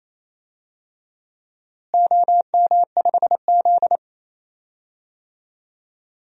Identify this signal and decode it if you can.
Morse code, heard as a keyed tone, about 27 wpm, 700 Hz. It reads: OM5Z
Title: DXCC-100 Call Signs - 4 Characters - Repeated 3x - 28wpm